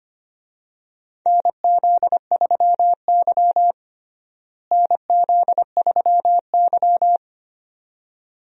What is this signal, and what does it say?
Morse code, heard as a keyed tone, about 25 wpm, 700 Hz. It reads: NZ3Y NZ3Y
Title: USA Call Signs - 4 Characters - Repeated 2x - 25wpm